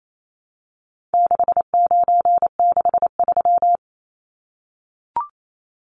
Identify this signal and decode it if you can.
Morse code, heard as a keyed tone, about 28 wpm, 700 Hz. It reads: 6963
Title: Numbers - 4-Digit 28wpm